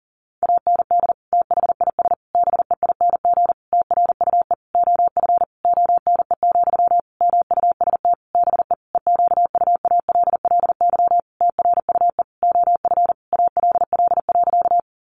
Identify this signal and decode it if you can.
Morse code, heard as a keyed tone, about 40 wpm, 700 Hz. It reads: AND THIS BEING TRUE OF ONE, MUST BE EQUALLY TRUE OF ALL.
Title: Sentences from Top 1400 Words 40wpm